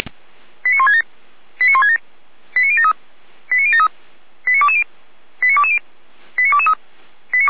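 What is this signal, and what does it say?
Five-tone selective call (ZVEI), heard as quick runs of five tones, tones of about 70 ms. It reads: 89167 89167 89073 89073 89220 89220 89223
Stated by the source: ZVEI five-tone selective call 89011, received by SDR.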